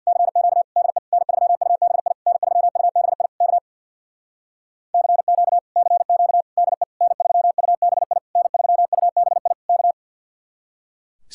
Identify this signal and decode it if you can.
Morse code, heard as a keyed tone, about 64 wpm, 700 Hz. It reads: CQ DE N3UBI N3UBI K CQ CQ DE N3UBI N3UBI K